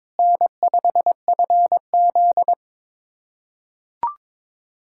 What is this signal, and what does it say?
Morse code, heard as a keyed tone, about 22 wpm, 700 Hz. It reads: N5FZ
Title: USA Callsign DE Callsign - 4 Characters - 22wpm